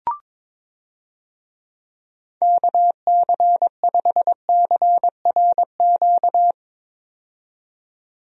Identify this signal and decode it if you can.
Morse code, heard as a keyed tone, about 22 wpm, 700 Hz. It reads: KC5CRQ
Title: USA Call Signs - 6 Characters - 22wpm